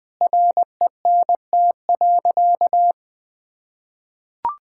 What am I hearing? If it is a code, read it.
Morse code, heard as a keyed tone, about 20 wpm, 700 Hz. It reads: RENT.
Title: Sentences from Top 1000 Words - Review Part 5 - 20wpm